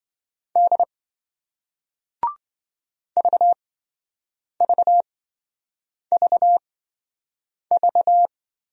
Morse code, heard as a keyed tone, about 27 wpm, spaced as for 7 wpm, 700 Hz. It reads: D VVVV